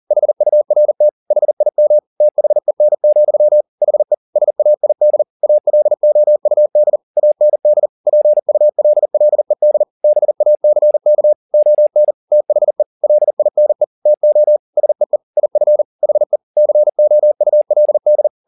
Morse code, heard as a keyed tone, about 40 wpm, 600 Hz. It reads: HURT HIM THEN, HE SAID ALOUD AND PULLED BACK ON THE LINE TO SEE IF HE COULD